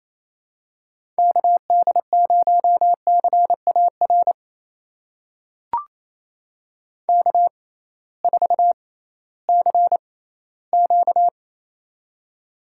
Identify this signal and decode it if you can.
Morse code, heard as a keyed tone, about 27 wpm, spaced as for 16 wpm, 700 Hz. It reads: KD0CAR K4CQ
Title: USA Call Signs - 6x Character Spacing - 28wpm